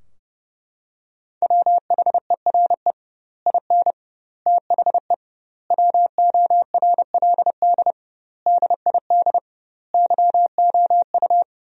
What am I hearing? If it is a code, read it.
Morse code, heard as a keyed tone, about 30 wpm, 700 Hz. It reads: WHERE IN THE WORLD DID YOU
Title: 2x Word-Spacing - Sentences from Top 200 Words - 30wpm